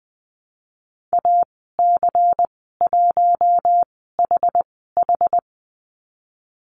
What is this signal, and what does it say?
Morse code, heard as a keyed tone, about 20 wpm, spaced as for 14 wpm, 700 Hz. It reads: AC1HH
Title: 161 Lesson 26g - Letter g - Call Signs 20wpm